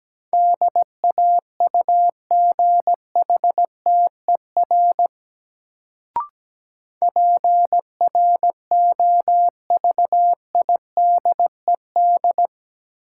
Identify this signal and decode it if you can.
Morse code, heard as a keyed tone, about 17 wpm, 700 Hz. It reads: DAUGHTER PROVIDED
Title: Sentences from Top 1500 Words - New Word Drill - 17wpm